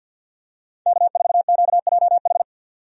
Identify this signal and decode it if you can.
Morse code, heard as a keyed tone, about 50 wpm, 700 Hz. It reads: K4QJH
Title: USA Call Signs - 5 Characters - 50wpm